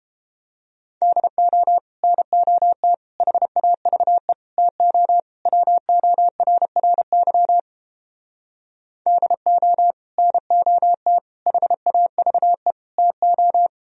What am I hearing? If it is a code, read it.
Morse code, heard as a keyed tone, about 29 wpm, 700 Hz. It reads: DO NOT HAVE TO WORRY DO NOT HAVE TO